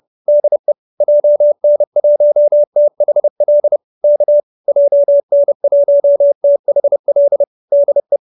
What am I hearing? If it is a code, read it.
Morse code, heard as a keyed tone, about 30 wpm, 570 Hz. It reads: DE JN1THL K JN1THL DE
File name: QSO309 30